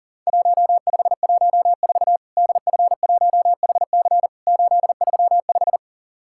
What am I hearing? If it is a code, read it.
Morse code, heard as a keyed tone, about 40 wpm, 700 Hz. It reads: 1514 DF1HC 835
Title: Worked All Europe DX Contest - QTC Exchange - 40wpm